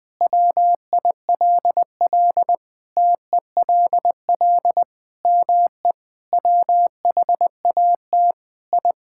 Morse code, heard as a keyed tone, about 20 wpm, 700 Hz. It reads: WILL TELL ME WHAT I
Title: Sentences from Top 1000 Words - Review Part 5 - 20wpm